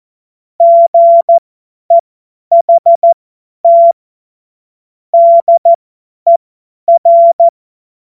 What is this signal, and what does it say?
Morse code, heard as a keyed tone, about 14 wpm, spaced as for 10 wpm, 680 Hz. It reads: GEHT DER
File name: Morsekurs / Witze / Witz 23. August 2022 50 F70